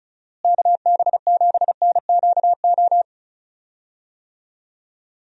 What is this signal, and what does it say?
Morse code, heard as a keyed tone, about 36 wpm, 700 Hz. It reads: KB7NQO
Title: USA Call Signs - 6 Characters - Repeated 3x - 35wpm